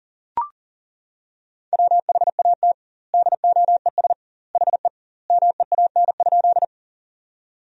Morse code, heard as a keyed tone, about 40 wpm, 700 Hz. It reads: WHAT DOES HE MEAN?
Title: 2x Word-Spacing - Sentences from Top 200 Words - 40wpm